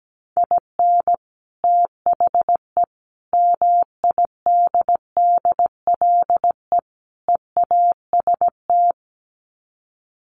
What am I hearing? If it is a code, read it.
Morse code, heard as a keyed tone, about 17 wpm, 700 Hz. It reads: IN THE MIDDLE EAST
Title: Sets of 4 Words 17wpm